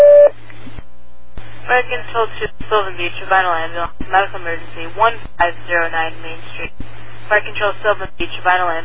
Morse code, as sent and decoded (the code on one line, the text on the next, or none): none